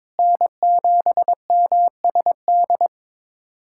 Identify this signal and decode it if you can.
Morse code, heard as a keyed tone, about 22 wpm, 700 Hz. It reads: N7MSD
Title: USA Callsign DE Callsign 22wpm